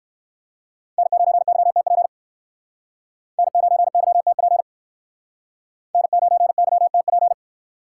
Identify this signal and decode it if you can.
Morse code, heard as a keyed tone, about 60 wpm, 700 Hz. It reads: N9YTP N9YTP N9YTP